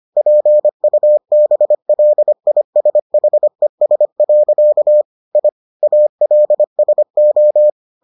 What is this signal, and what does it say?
Morse code, heard as a keyed tone, about 25 wpm, 600 Hz. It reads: PUBLISHES. I ALSO